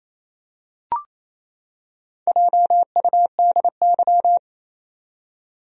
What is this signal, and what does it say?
Morse code, heard as a keyed tone, about 28 wpm, 700 Hz. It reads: JUDY
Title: USA Common Names - Max 5 Letters - 28wpm